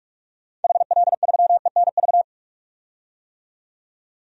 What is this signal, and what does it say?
Morse code, heard as a keyed tone, about 46 wpm, 700 Hz. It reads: HL3ENV
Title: DXCC-100 Call Signs - 6 Characters - 45wpm